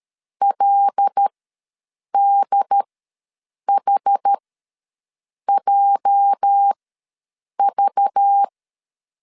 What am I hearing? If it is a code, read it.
Morse code, heard as a keyed tone, about 13 wpm, spaced as for 7 wpm, 800 Hz. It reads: LDHJV
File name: LEZIONI / 8 YJ45 / 8 YJ4 ws 12wpm 3249